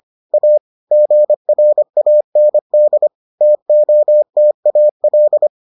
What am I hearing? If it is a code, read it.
Morse code, heard as a keyed tone, about 25 wpm, 600 Hz. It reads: A GRAND TOTAL